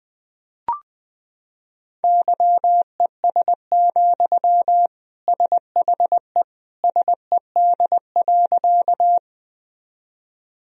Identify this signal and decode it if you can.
Morse code, heard as a keyed tone, about 21 wpm, 700 Hz. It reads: YES, SHE SED.
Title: Sentences from Top 500 Words and Abbreviations 20wpm